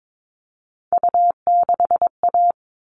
Morse code, heard as a keyed tone, about 22 wpm, 700 Hz. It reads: U6A
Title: All Cut Numbers - 3-Digit 22wpm